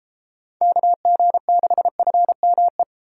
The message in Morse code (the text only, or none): KG6FME